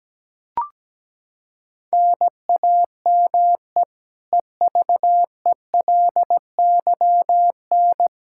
Morse code, heard as a keyed tone, about 17 wpm, 700 Hz. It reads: NAME EVELYN
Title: Practice QSO - Exchange 1 - Beginning - 17wpm